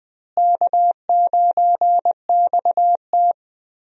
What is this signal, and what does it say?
Morse code, heard as a keyed tone, about 21 wpm, 700 Hz. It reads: K9XT
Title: USA Call Signs - 4 Characters - Repeated 2x - 20wpm